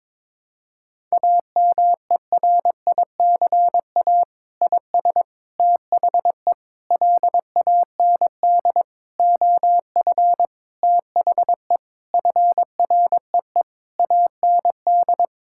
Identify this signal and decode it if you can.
Morse code, heard as a keyed tone, about 22 wpm, 700 Hz. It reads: AMERICA IS THE LAND OF THE FREE AND
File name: practice / 22wpm / Sentences with 8-Word Phrases - 22wpm